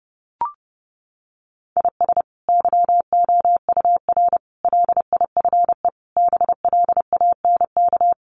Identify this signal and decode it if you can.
Morse code, heard as a keyed tone, about 30 wpm, 700 Hz. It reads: IS YOUR LIFE BLANK
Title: Sentences from Top 1500 Words 30wpm